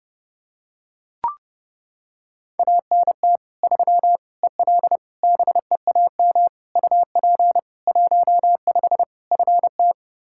Morse code, heard as a keyed tone, about 30 wpm, 700 Hz. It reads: ANT 3 EL BEAM UP 15 FT
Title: Practice QSO - Exchange 2 - Info - 30wpm